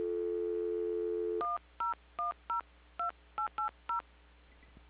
DTMF, heard as telephone touch tones, tones of about 130 ms. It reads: 10102880